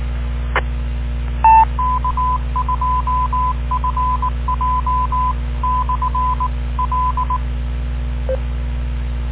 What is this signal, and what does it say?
Morse code, heard as a keyed tone, about 19 wpm, 1010 Hz, with noise. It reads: E K2FJ/L